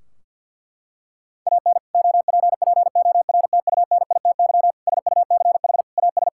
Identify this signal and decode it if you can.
Morse code, heard as a keyed tone, about 51 wpm, 700 Hz. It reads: AN OPPORTUNITY SUCH AS
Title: Sentences from Top 1600 Words 50wpm